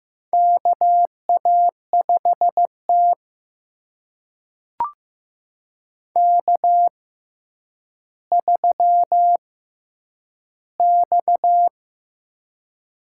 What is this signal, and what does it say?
Morse code, heard as a keyed tone, about 15 wpm, spaced as for 9 wpm, 700 Hz. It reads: KA5T K3X
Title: USA Call Signs - 4 Characters - 6x Character Spacing - 15wpm